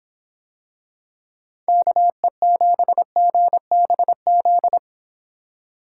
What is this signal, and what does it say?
Morse code, heard as a keyed tone, about 24 wpm, 700 Hz. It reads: KE7GBZ